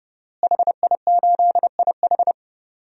Morse code, heard as a keyed tone, about 32 wpm, 700 Hz. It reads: HI8IH
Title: DXCC-100 Call Signs - 5 Characters - Repeated 2x - 30wpm